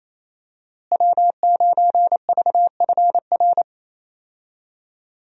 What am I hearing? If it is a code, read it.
Morse code, heard as a keyed tone, about 28 wpm, 700 Hz. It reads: W9VFR
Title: USA Call Signs - 5 Characters - Repeated 3x - 28wpm